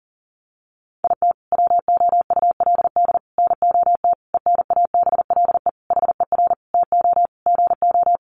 Morse code, heard as a keyed tone, about 40 wpm, 700 Hz. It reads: IT WOULD NOT ENABLE HER TO GO
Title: Sentences from Top 1500 Words 40wpm